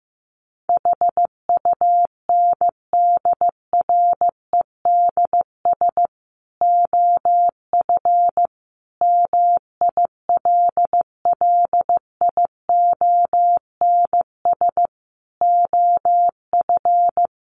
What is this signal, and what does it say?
Morse code, heard as a keyed tone, about 14 wpm, 700 Hz. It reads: HUNDREDS OF MILLIONS OF
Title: Sets of 5 Words 15wpm